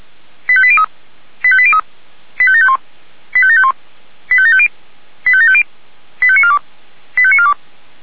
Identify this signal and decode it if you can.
Five-tone selective call (ZVEI), heard as quick runs of five tones, tones of about 70 ms. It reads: 85803 85803 85751 85751 85750 85750 85942 85942
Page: SDR/ZVEI